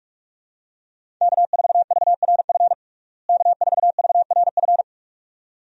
Morse code, heard as a keyed tone, about 42 wpm, 700 Hz. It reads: K4VRF K4VRF
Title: USA Call Signs - 5 Characters - Repeated 2x - 45wpm